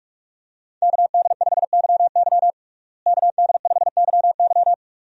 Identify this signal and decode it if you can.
Morse code, heard as a keyed tone, about 42 wpm, 700 Hz. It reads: KD5YY KD5YY
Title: USA Call Signs - 5 Characters - Repeated 2x - 45wpm